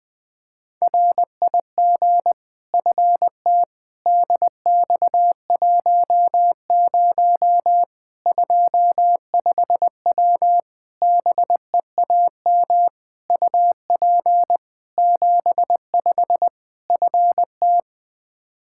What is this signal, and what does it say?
Morse code, heard as a keyed tone, about 20 wpm, 700 Hz. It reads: RIG FT DX10 25W BEAM UP 75 FT
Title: Practice QSO - Exchange 2 - Info - 20wpm